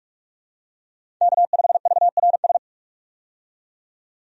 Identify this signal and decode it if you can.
Morse code, heard as a keyed tone, about 42 wpm, 700 Hz. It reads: K5VRS
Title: USA Call Signs - 5 Characters - 45wpm